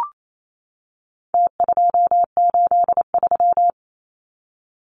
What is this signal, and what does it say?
Morse code, heard as a keyed tone, about 28 wpm, 700 Hz. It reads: T283
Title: Numbers - Contest Style - 4-Digit 28wpm